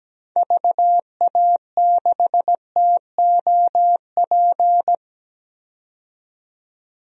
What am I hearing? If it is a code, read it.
Morse code, heard as a keyed tone, about 17 wpm, 700 Hz. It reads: VA6TOP
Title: DXCC-100 Call Signs - 6 Characters - 17wpm